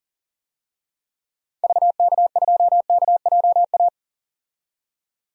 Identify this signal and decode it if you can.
Morse code, heard as a keyed tone, about 40 wpm, 700 Hz. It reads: VK2KJA
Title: DXCC-100 Call Signs - 6 Characters - 40wpm